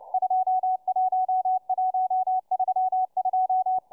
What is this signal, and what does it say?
Morse code, heard as a keyed tone, about 29 wpm, 750 Hz, with radio noise. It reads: 21132